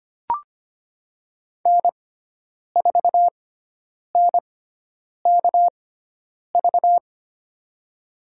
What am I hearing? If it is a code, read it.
Morse code, heard as a keyed tone, about 25 wpm, spaced as for 9 wpm, 700 Hz. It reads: N4NKV
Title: USA Call Signs - 5 Characters - 6x Character Spacing - 25wpm